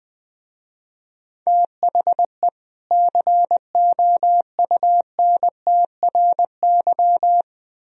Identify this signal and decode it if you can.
Morse code, heard as a keyed tone, about 20 wpm, 700 Hz. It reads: THE COUNTRY
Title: Speed-Racing - Sets of 2 Words 20wpm